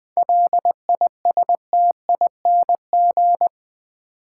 Morse code, heard as a keyed tone, about 20 wpm, 700 Hz. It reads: LISTING
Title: 7 Letter Words - First 3 Letters Given - 20wpm